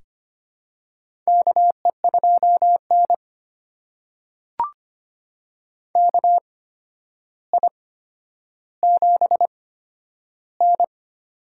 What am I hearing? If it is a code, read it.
Morse code, heard as a keyed tone, about 25 wpm, spaced as for 11 wpm, 700 Hz. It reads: KE2N KI7N
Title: USA Call Signs - 8x Character Spacing - 25wpm